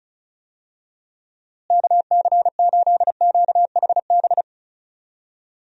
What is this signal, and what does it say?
Morse code, heard as a keyed tone, about 34 wpm, 700 Hz. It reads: KC8QHB
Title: USA Call Signs - 6 Characters - Repeated 3x - 35wpm